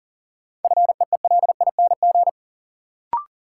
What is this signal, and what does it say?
Morse code, heard as a keyed tone, about 40 wpm, 700 Hz. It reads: FEELING
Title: Sentences from Top 1000 Words - Review Part 3 - New Word Drill - 40wpm